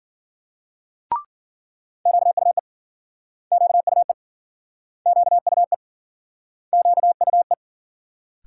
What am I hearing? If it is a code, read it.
Morse code, heard as a keyed tone, about 53 wpm, 700 Hz. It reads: QUE QUE QUE QUE